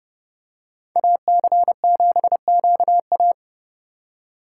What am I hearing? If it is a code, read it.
Morse code, heard as a keyed tone, about 28 wpm, 700 Hz. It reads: AC7QA